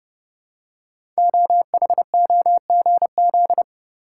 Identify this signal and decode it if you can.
Morse code, heard as a keyed tone, about 30 wpm, 700 Hz. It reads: OHOGZ
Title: Code Groups - 5 Letters - 30wpm